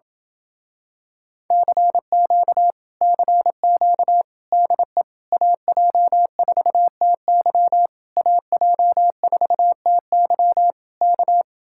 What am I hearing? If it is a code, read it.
Morse code, heard as a keyed tone, about 28 wpm, 700 Hz. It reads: CQ CQ DE AJ4TY AJ4TY K